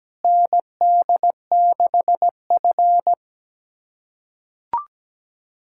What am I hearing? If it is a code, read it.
Morse code, heard as a keyed tone, about 17 wpm, 700 Hz. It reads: ND6F
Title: USA Callsign DE Callsign - 4 Characters - 17wpm